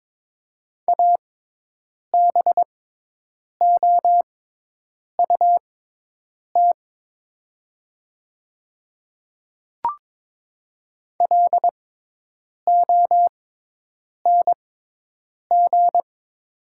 Morse code, heard as a keyed tone, about 22 wpm, spaced as for 8 wpm, 700 Hz. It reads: ABOUT LONG
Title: 6x Character Spacing - Top 100 Words - 22wpm